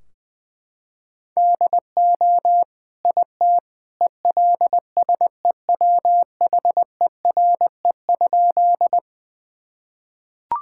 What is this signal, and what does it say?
Morse code, heard as a keyed tone, about 19 wpm, 700 Hz. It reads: DO IT ELSEWHERE?
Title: Sentences from Top 1600 Words 20wpm